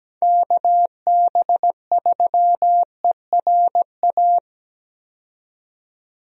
Morse code, heard as a keyed tone, about 17 wpm, 700 Hz. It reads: KB3ERA
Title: USA Call Signs - 6 Characters - 17wpm